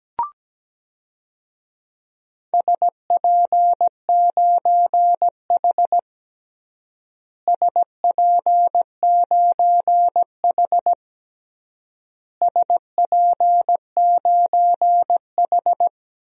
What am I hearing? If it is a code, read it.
Morse code, heard as a keyed tone, about 17 wpm, 700 Hz. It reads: SP9H SP9H SP9H
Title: DXCC-100 Call Signs - 4 Characters - Repeated 3x - 17wpm